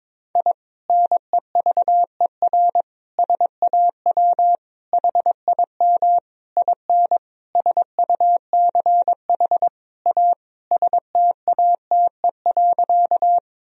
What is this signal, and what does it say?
Morse code, heard as a keyed tone, about 22 wpm, 700 Hz. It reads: I NEVER SAW HIM IN SUCH A STATE.
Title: Sentences from Top 1000 Words - Review Part 2 - 22wpm